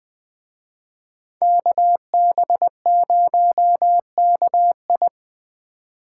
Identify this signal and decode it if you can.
Morse code, heard as a keyed tone, about 21 wpm, 700 Hz. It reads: KB0KI
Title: Speed Racing - USA Call Signs 17wpm